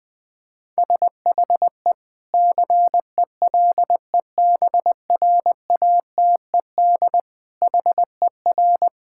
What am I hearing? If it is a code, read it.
Morse code, heard as a keyed tone, about 20 wpm, 700 Hz. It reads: SHE CELEBRATED HER
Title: Difficult Sentences with Hints - 20wpm